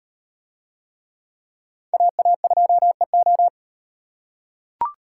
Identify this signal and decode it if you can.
Morse code, heard as a keyed tone, about 36 wpm, 700 Hz. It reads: AA2EO